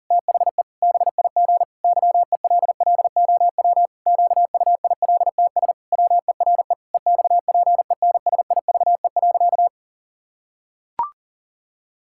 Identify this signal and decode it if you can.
Morse code, heard as a keyed tone, about 40 wpm, 700 Hz. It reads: THE BIG YELLOW QUILTS WERE EXPENSIVE.
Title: Pangram Challenge v2 - 40wpm